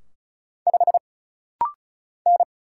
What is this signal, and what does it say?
Morse code, heard as a keyed tone, about 35 wpm, 700 Hz. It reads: 5 N
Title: Single Letter-Number Alt - Rapid-Fire 35wpm